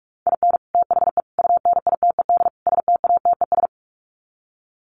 Morse code, heard as a keyed tone, about 44 wpm, 700 Hz. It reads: IN THE UNITED STATES
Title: Sets of 4 Words 45wpm